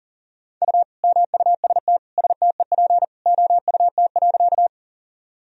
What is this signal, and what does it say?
Morse code, heard as a keyed tone, about 40 wpm, 700 Hz. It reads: U MUST STEP OUT.
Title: Sentences from Top 400 Words and Abbreviations 40wpm